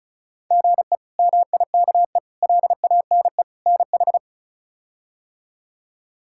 Morse code, heard as a keyed tone, about 36 wpm, 700 Hz. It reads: GE MIKE LANE NH
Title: K1USN SST QSO Practice - 35wpm